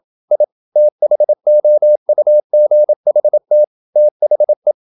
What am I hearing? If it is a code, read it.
Morse code, heard as a keyed tone, about 27 wpm, 600 Hz. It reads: I THOUGHT THE
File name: Never 0027wpm